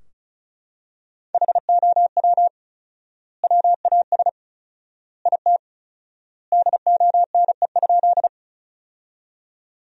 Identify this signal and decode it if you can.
Morse code, heard as a keyed tone, about 36 wpm, 700 Hz. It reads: HOW WAS IT DONE?
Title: 4x Word-Spacing - Sentences from Top 500 Words - 35wpm